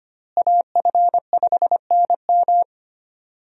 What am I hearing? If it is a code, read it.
Morse code, heard as a keyed tone, about 25 wpm, 700 Hz. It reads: AF5NM
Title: USA Call Signs - 5 Characters - Repeated 2x - 25wpm